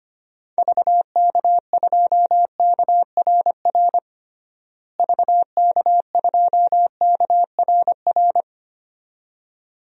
Morse code, heard as a keyed tone, about 25 wpm, 700 Hz. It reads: VK2KRR VK2KRR
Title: DXCC-100 Call Signs - 6 Characters - Repeated 2x - 25wpm